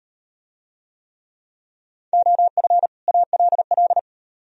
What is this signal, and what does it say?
Morse code, heard as a keyed tone, about 35 wpm, 700 Hz. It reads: OF ALL